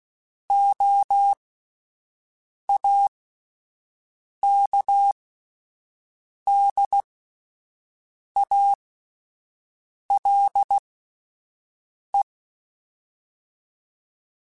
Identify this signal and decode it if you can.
Morse code, heard as a keyed tone, about 16 wpm, spaced as for 5 wpm, 800 Hz. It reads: OAKDALE